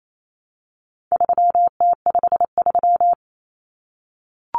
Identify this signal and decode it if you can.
Morse code, heard as a keyed tone, about 28 wpm, 700 Hz. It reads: 3T53
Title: Common Cut Numbers - 4-Digit 28wpm